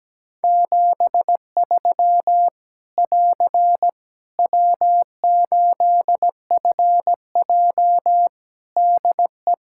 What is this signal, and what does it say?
Morse code, heard as a keyed tone, about 17 wpm, 700 Hz. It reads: 73 <AR> W8FJ DE
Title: Practice QSO - Exchange 4 - Ending - 17wpm